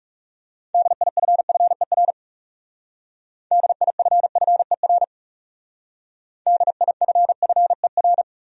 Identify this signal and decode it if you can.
Morse code, heard as a keyed tone, about 38 wpm, 700 Hz. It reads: DIFFER DIFFER DIFFER